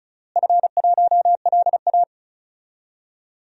Morse code, heard as a keyed tone, about 36 wpm, 700 Hz. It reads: F1LA
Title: DXCC-100 Call Signs - 4 Characters - Repeated 3x - 35wpm